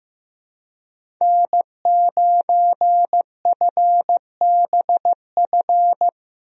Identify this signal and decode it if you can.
Morse code, heard as a keyed tone, about 15 wpm, 700 Hz. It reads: N9FBF
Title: USA Call Signs - 5 Characters - Repeated 2x - 15wpm